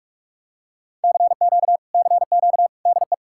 Morse code, heard as a keyed tone, about 46 wpm, 700 Hz. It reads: CQ CQ DE